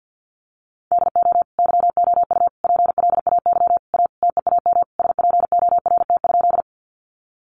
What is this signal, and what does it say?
Morse code, heard as a keyed tone, about 50 wpm, 700 Hz. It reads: DO YOU PLAY A TEAM SPORT?
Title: Sentences from Top 1500 Words 50wpm